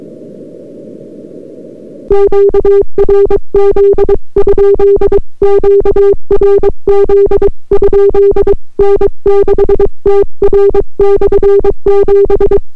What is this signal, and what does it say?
Morse code, heard as a keyed tone, about 22 wpm, 400 Hz. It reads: QRZ? QRZ? N6TR/7